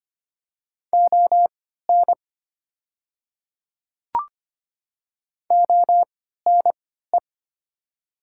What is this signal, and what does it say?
Morse code, heard as a keyed tone, about 25 wpm, spaced as for 14 wpm, 700 Hz. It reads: ON ONE
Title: 3x Character Spacing - Top 200 Words - 25wpm